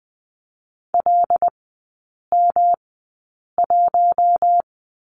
Morse code, heard as a keyed tone, about 20 wpm, spaced as for 8 wpm, 700 Hz. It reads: LM1
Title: 107 Lesson 18e - Letter m - Three Characters 20wpm